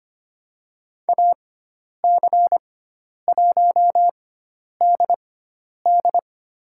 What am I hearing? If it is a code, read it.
Morse code, heard as a keyed tone, about 25 wpm, spaced as for 10 wpm, 700 Hz. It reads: AC1DD
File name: Lesson 16g - Letter c - Call Signs 25wpm